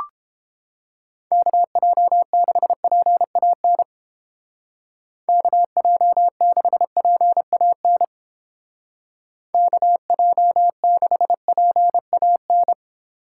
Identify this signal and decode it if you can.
Morse code, heard as a keyed tone, about 31 wpm, 700 Hz. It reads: KJ6PAN KJ6PAN KJ6PAN